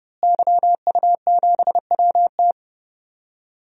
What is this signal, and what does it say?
Morse code, heard as a keyed tone, about 30 wpm, 700 Hz. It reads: YU7WT
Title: DXCC-100 Call Signs - 5 Characters - Repeated 2x - 30wpm